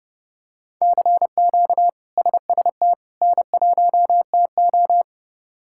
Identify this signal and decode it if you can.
Morse code, heard as a keyed tone, about 32 wpm, 700 Hz. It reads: CQ SST N1TO
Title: K1USN SST QSO Practice - 30wpm